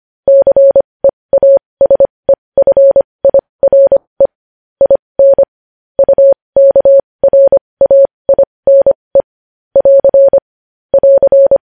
Morse code, heard as a keyed tone, about 25 wpm, spaced as for 20 wpm, 550 Hz. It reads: CEASEFIRE IN UKRAINE <AR> <AR>
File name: podcast / News headlines-1x-25x20